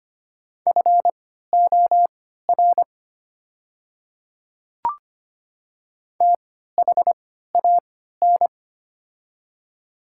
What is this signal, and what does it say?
Morse code, heard as a keyed tone, about 25 wpm, spaced as for 14 wpm, 700 Hz. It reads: FOR THAN
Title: 3x Character Spacing - Top 200 Words - 25wpm